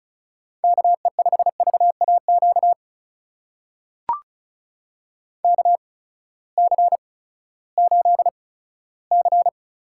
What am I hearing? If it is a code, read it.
Morse code, heard as a keyed tone, about 36 wpm, spaced as for 18 wpm, 700 Hz. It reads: KE5VAQ KC8C
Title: USA Call Signs - 8x Character Spacing - 35wpm